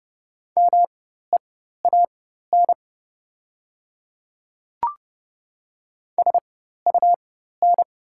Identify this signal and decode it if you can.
Morse code, heard as a keyed tone, about 30 wpm, spaced as for 14 wpm, 700 Hz. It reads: MEAN SUN
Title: 4x Character Spacing - Top 300 Words - 30wpm